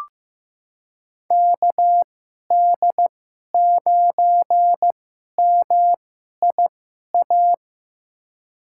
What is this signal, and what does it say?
Morse code, heard as a keyed tone, about 15 wpm, spaced as for 11 wpm, 700 Hz. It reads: KD9MIA
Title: USA Call Signs - 2x Character Spacing - 15wpm